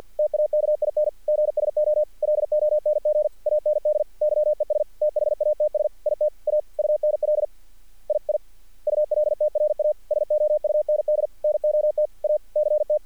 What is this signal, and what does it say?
Morse code, heard as a keyed tone, about 50 wpm, 600 Hz. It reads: TAKING SO LONG AND YES THATS IT A UNL II ULTRASOUND NOT A CT
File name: archived / CRSnet